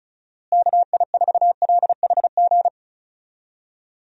Morse code, heard as a keyed tone, about 34 wpm, 700 Hz. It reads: KI4LHG
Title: USA Call Signs - 6 Characters - Repeated 2x - 35wpm